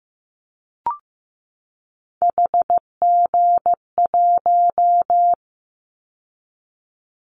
Morse code, heard as a keyed tone, about 15 wpm, 700 Hz. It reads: HG1
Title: Callsign Prefixes for Top 25 DXCC - 15wpm